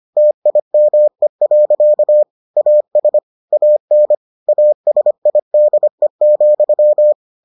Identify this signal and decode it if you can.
Morse code, heard as a keyed tone, about 25 wpm, 600 Hz. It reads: TIME. AS AN ASIDE,